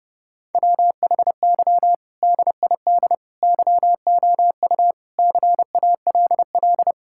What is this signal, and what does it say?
Morse code, heard as a keyed tone, about 30 wpm, 700 Hz. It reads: WHY DID YOU CALL
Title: Sentences from Top 1000 Words - Review Part 6 - 30wpm